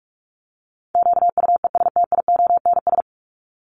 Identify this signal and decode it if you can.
Morse code, heard as a keyed tone, about 44 wpm, 700 Hz. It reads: QUESTIONS
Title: ES:S Suffix - Top 100 Words with Base Words 45wpm